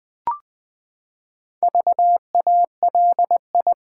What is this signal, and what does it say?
Morse code, heard as a keyed tone, about 20 wpm, 700 Hz. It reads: VALI
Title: Build A Word - 4 Letters At A Time - 8-Letter Words - 20wpm